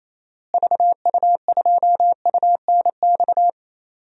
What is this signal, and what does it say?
Morse code, heard as a keyed tone, about 27 wpm, 700 Hz. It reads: VU2UNX
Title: DXCC-100 Call Signs - 6 Characters - Repeated 3x - 28wpm